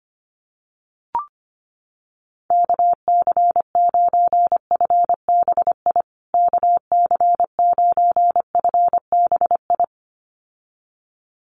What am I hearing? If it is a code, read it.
Morse code, heard as a keyed tone, about 25 wpm, 700 Hz. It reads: KC9FBI KC9FBI
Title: USA Call Signs Repeated 2x 25wpm